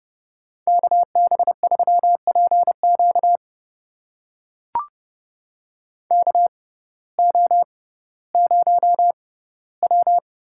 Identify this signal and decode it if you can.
Morse code, heard as a keyed tone, about 30 wpm, spaced as for 17 wpm, 700 Hz. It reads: KB3PQ KO0W
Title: USA Call Signs - 6x Character Spacing - 30wpm